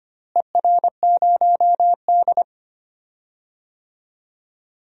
Morse code, heard as a keyed tone, about 25 wpm, 700 Hz. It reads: ER0D
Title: DXCC-100 Call Signs - 4 Characters - Repeated 3x - 25wpm